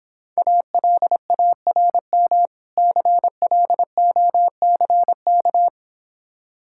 Morse code, heard as a keyed tone, about 26 wpm, 700 Hz. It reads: ALARM CLOCK